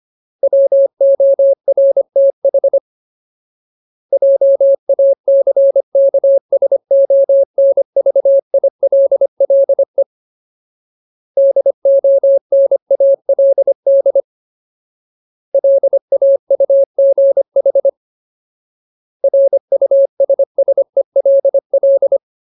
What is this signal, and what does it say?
Morse code, heard as a keyed tone, about 25 wpm, 550 Hz. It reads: WORTH JACKSONVILLE DONALD LAUGH RUSSELL
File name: Ultimate Word List / 25WPM Ultimate